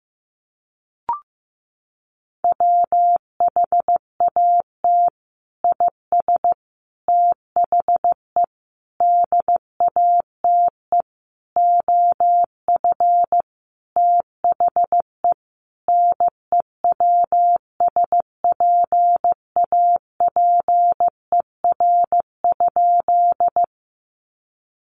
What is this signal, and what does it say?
Morse code, heard as a keyed tone, about 15 wpm, 700 Hz. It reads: WHAT IS THE DATE OF THE NEWSPAPER?
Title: Sentences from Top 1100 Words 15wpm